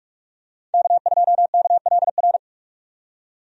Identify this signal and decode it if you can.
Morse code, heard as a keyed tone, about 46 wpm, 700 Hz. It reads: K2KLR
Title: USA Call Signs - 5 Characters - 45wpm